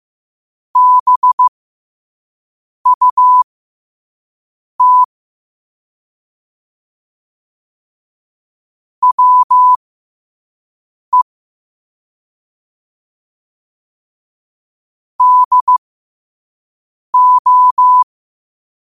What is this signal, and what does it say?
Morse code, heard as a keyed tone, about 15 wpm, spaced as for 5 wpm, 1000 Hz. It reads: BUT WE DO
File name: qotdfiles / 5wpm